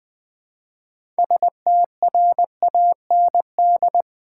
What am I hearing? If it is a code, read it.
Morse code, heard as a keyed tone, about 21 wpm, 700 Hz. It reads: STRAND
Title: 6 Letter Words - First 3 Letters Given - 20wpm